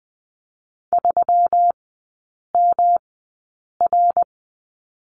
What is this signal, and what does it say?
Morse code, heard as a keyed tone, about 20 wpm, spaced as for 8 wpm, 700 Hz. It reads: 3MR
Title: 133 Lesson 22e - Question Mark - Three Characters 20wpm